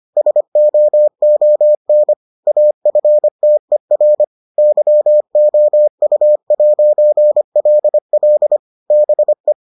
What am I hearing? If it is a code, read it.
Morse code, heard as a keyed tone, about 25 wpm, 600 Hz. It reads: SOON AFTER YOU'LL BE